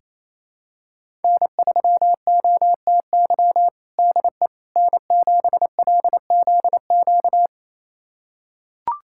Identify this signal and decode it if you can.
Morse code, heard as a keyed tone, about 28 wpm, 700 Hz. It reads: N3OTY DE N7LZQ
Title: USA Callsign DE Callsign - 5 Characters - 28wpm